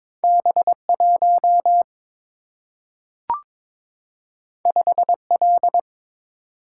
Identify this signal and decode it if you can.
Morse code, heard as a keyed tone, about 22 wpm, 700 Hz. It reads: B1 5L
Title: Single-Letter Single-Number 22wpm